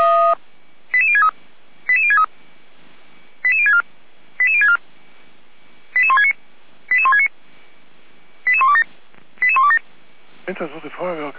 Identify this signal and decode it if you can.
Five-tone selective call (ZVEI), heard as quick runs of five tones, tones of about 70 ms, with speech. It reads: 88063 88063 88064 88064 88169 88169 88127 88127